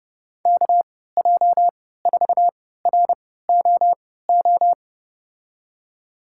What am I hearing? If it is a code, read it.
Morse code, heard as a keyed tone, about 30 wpm, spaced as for 17 wpm, 700 Hz. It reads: KJ4ROO
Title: USA Call Signs - 6 Characters - 3x Character Spacing - 30wpm